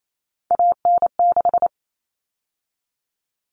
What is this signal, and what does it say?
Morse code, heard as a keyed tone, about 28 wpm, 700 Hz. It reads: AN6
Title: All Cut Numbers - 3-Digit 28wpm